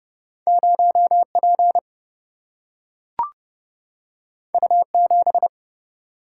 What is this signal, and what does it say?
Morse code, heard as a keyed tone, about 30 wpm, 700 Hz. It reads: 0 P U 7
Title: Single-Letter Single-Number 30wpm